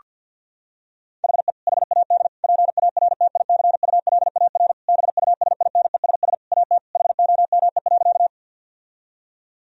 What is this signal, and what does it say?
Morse code, heard as a keyed tone, about 50 wpm, 700 Hz. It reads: HE HAD PARTICULAR BUSINESS AT HOME.
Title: Sentences from Top 1000 Words - Review Part 4 - 50wpm